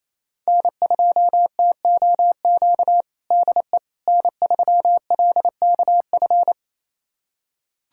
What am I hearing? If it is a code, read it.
Morse code, heard as a keyed tone, about 28 wpm, 700 Hz. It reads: N2TOQ DE N3LKF
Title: USA Callsign DE Callsign - 5 Characters - 28wpm